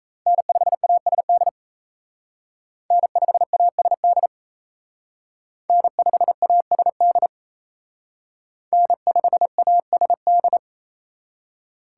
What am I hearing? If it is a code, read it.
Morse code, heard as a keyed tone, about 35 wpm, 700 Hz. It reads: N5ASD N5ASD N5ASD N5ASD